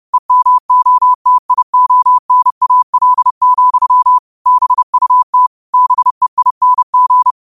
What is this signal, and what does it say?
Morse code, heard as a keyed tone, about 30 wpm, 1000 Hz. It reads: EMOTIONAL, BUT BEING
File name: qotdfiles / 30wpm